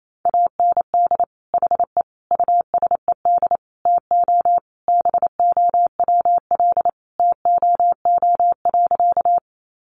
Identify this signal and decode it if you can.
Morse code, heard as a keyed tone, about 28 wpm, 700 Hz. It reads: AND HE USED TO BOWL TOO.
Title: Sentences from Top 1200 Words 28wpm